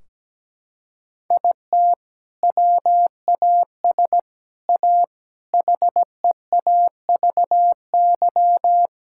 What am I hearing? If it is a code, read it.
Morse code, heard as a keyed tone, about 17 wpm, 700 Hz. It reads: IT WAS A HEAVY
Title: Sentences from Top 1600 Words 17wpm